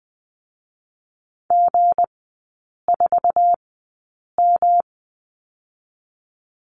Morse code, heard as a keyed tone, about 20 wpm, spaced as for 8 wpm, 690 Hz. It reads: G4M
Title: 212 Lesson 34e - Number 8 - Three Characters 20wpm